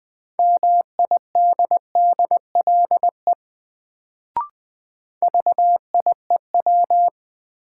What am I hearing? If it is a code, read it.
Morse code, heard as a keyed tone, about 21 wpm, 700 Hz. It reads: MIDDLE VIEW
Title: Sentences from Top 900 Words - New Word Drill - 20wpm